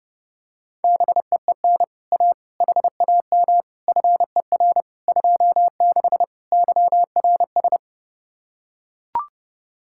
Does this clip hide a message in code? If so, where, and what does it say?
Yes: Morse code from 0.8 to 7.8 s, heard as a keyed tone, about 30 wpm, 700 Hz. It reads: BEEN A HAM FER 26 YRS